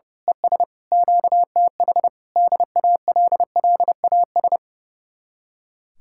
Morse code, heard as a keyed tone, about 28 wpm, 700 Hz. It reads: ES QTH DALLAS
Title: Speed-Racing Common QSO Phrases - 30wpm